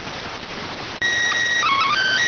SSTV (slow-scan television), PD 290.